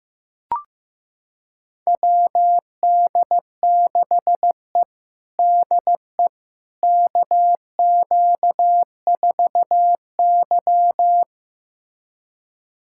Morse code, heard as a keyed tone, about 15 wpm, 700 Hz. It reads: WD6E DE KQ4Y
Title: USA Callsign DE Callsign - 4 Characters - 15wpm